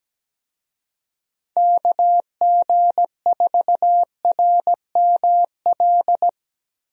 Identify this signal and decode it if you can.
Morse code, heard as a keyed tone, about 17 wpm, 700 Hz. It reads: KG4RML
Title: USA Call Signs - 6 Characters - Repeated 2x - 17wpm